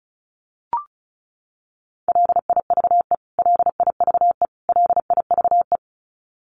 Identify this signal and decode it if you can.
Morse code, heard as a keyed tone, about 34 wpm, 700 Hz. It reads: LIVE LIVE LIVE
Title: Top 200 Words Repeated 35wpm